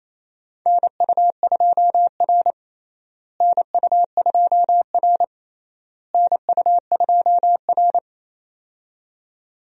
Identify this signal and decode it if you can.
Morse code, heard as a keyed tone, about 28 wpm, 700 Hz. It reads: NU2R NU2R NU2R
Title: USA Call Signs - 4 Characters - Repeated 3x - 28wpm